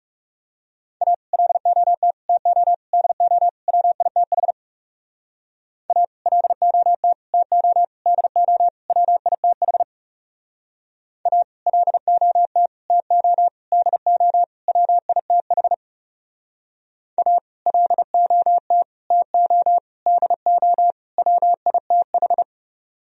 Morse code, heard as a keyed tone, about 40 wpm, 700 Hz. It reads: A LOT TO DO WITH A LOT TO DO WITH A LOT TO DO WITH A LOT TO DO WITH